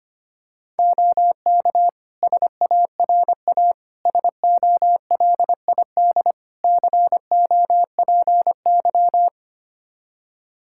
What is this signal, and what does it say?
Morse code, heard as a keyed tone, about 25 wpm, 700 Hz. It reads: OK SARA SOLID COPY
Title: Practice QSO - Exchange 3 - More Info - 25wpm